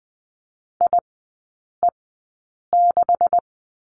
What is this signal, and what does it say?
Morse code, heard as a keyed tone, about 20 wpm, spaced as for 8 wpm, 700 Hz. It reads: IE6
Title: 127 Lesson 21e - Number 6 - Three Characters 20wpm